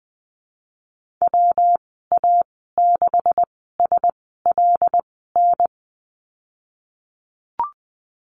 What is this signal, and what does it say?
Morse code, heard as a keyed tone, about 20 wpm, spaced as for 15 wpm, 700 Hz. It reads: WA6SLN
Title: 147 Lesson 24g - Letter y - Call Signs 20wpm